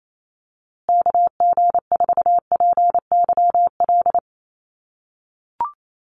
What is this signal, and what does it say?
Morse code, heard as a keyed tone, about 28 wpm, 700 Hz. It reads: KG4PYL
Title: USA Call Signs Repeated 2x 28wpm